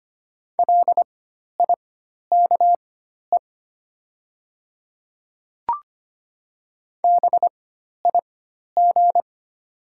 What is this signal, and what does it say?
Morse code, heard as a keyed tone, about 25 wpm, spaced as for 12 wpm, 690 Hz. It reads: LIKE BIG
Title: 4x Character Spacing - Top 500 Words - 25wpm